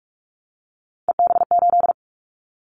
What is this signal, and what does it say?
Morse code, heard as a keyed tone, about 46 wpm, 700 Hz. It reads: EB8
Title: Callsign Prefixes for Top 100 DXCC - 45wpm